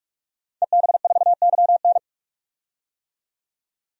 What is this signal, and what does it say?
Morse code, heard as a keyed tone, about 46 wpm, 700 Hz. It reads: EB4YN
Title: DXCC-100 Call Signs - 5 Characters - 45wpm